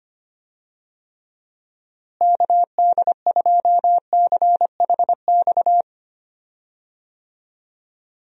Morse code, heard as a keyed tone, about 25 wpm, 700 Hz. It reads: KD2CHX
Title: USA Call Signs - 6 Characters - Repeated 2x - 25wpm